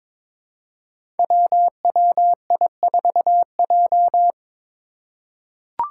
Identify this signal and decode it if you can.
Morse code, heard as a keyed tone, about 22 wpm, 700 Hz. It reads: WWI4J
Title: Five Challenging Letter-Number Combinations 22wpm